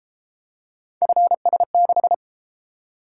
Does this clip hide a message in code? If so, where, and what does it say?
Yes: Morse code from 1.0 to 2.1 s, heard as a keyed tone, about 31 wpm, 700 Hz. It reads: FS6